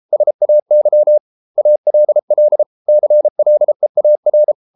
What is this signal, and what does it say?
Morse code, heard as a keyed tone, about 33 wpm, 600 Hz. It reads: SAY ALL CLEAR